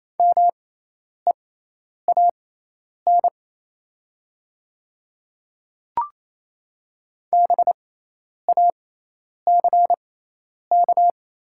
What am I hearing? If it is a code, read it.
Morse code, heard as a keyed tone, about 27 wpm, spaced as for 10 wpm, 700 Hz. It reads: MEAN BACK
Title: 6x Character Spacing - Top 500 Words - 28wpm